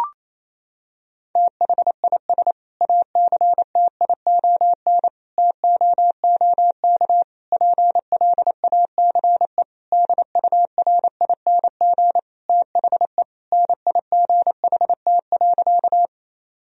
Morse code, heard as a keyed tone, about 28 wpm, 700 Hz. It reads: THIS ACTION TOOK PLACE DURING THE NIGHT.
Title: Sentences from Top 1000 Words - Review Part 4 - 28wpm